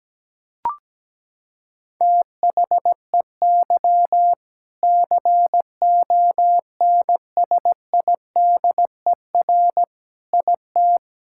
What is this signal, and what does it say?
Morse code, heard as a keyed tone, about 17 wpm, 700 Hz. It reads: THEY CONSIDER IT
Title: Sentences from Top 1000 Words - Review Part 5 - 17wpm